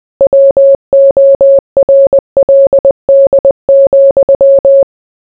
Morse code, heard as a keyed tone, about 20 wpm, 550 Hz. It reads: WORLD,